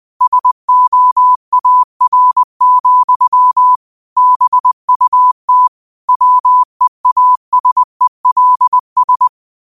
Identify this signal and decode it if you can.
Morse code, heard as a keyed tone, about 20 wpm, 1000 Hz. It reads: SOAR, BUT WEASELS
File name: qotdfiles / 20wpm